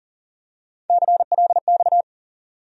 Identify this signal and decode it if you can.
Morse code, heard as a keyed tone, about 40 wpm, 700 Hz. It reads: CLX
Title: USA Call Sign Postfixes - 3 Letters 40wpm